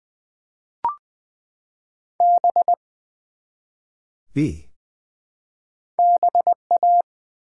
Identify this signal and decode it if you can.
Morse code, heard as a keyed tone, about 20 wpm, 700 Hz. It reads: B BA
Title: Build A Word - 1 Letter At A Time - 3 to 5-Letter Words - 20wpm